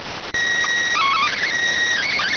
SSTV (slow-scan television), PD 290.